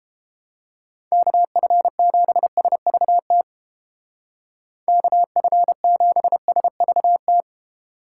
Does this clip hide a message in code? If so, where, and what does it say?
Yes: Morse code from 1.1 to 7.4 s, heard as a keyed tone, about 32 wpm, 700 Hz. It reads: KF7SVT KF7SVT